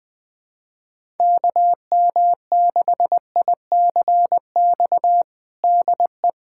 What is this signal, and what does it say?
Morse code, heard as a keyed tone, about 20 wpm, 700 Hz. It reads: KM6ICX DE
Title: USA Callsign DE Callsign - 6 Characters - 20wpm